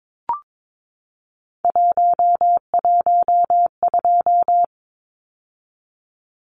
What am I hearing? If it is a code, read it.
Morse code, heard as a keyed tone, about 22 wpm, 700 Hz. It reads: 112
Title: Numbers - 3-Digit 22wpm